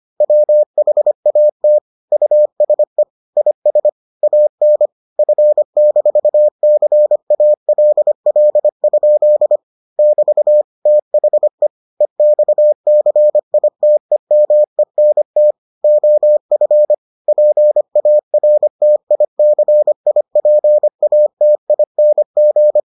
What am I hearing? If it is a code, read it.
Morse code, heard as a keyed tone, about 25 wpm, 600 Hz. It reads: WHAT USE IS AN F-CALL? <BT> THE EXCITEMENT OF PARTICIPATING